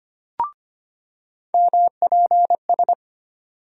Morse code, heard as a keyed tone, about 25 wpm, 700 Hz. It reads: MPS
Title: USA Call Sign Postfixes - 3 Letters 25wpm